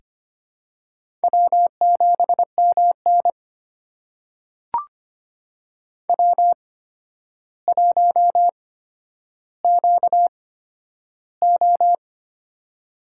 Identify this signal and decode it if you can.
Morse code, heard as a keyed tone, about 25 wpm, spaced as for 11 wpm, 700 Hz. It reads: W7MN W1QO
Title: USA Call Signs - 4 Characters - 8x Character Spacing - 25wpm